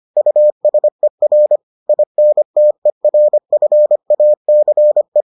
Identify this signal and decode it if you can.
Morse code, heard as a keyed tone, about 25 wpm, 600 Hz. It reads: USER INTERFACE